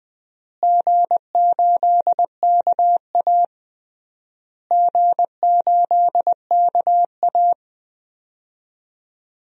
Morse code, heard as a keyed tone, about 20 wpm, 700 Hz. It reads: G8KA G8KA
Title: DXCC-100 Call Signs - 4 Characters - Repeated 3x - 20wpm